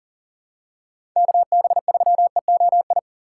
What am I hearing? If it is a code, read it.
Morse code, heard as a keyed tone, about 40 wpm, 700 Hz. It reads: KB3EOI
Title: USA Call Signs - 6 Characters - 6x Character Spacing - 40wpm